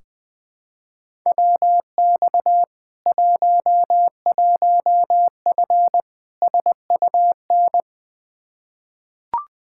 Morse code, heard as a keyed tone, about 20 wpm, 700 Hz. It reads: WX 11F SUN
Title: Practice QSO - Exchange 2 - Info - 20wpm